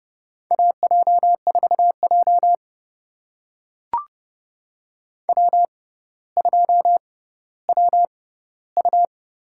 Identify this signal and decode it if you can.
Morse code, heard as a keyed tone, about 32 wpm, spaced as for 16 wpm, 700 Hz. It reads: AJ4J W2WU
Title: USA Call Signs - 4 Characters - 6x Character Spacing - 30wpm